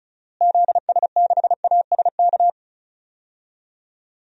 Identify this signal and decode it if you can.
Morse code, heard as a keyed tone, about 36 wpm, 700 Hz. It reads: ZS6ASK
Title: DXCC-100 Call Signs - 6 Characters - Repeated 3x - 35wpm